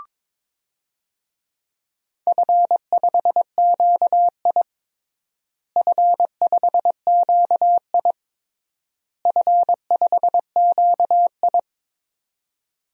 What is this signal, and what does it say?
Morse code, heard as a keyed tone, about 22 wpm, 700 Hz. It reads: F5QI F5QI F5QI
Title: DXCC-100 Call Signs - 4 Characters - Repeated 3x - 22wpm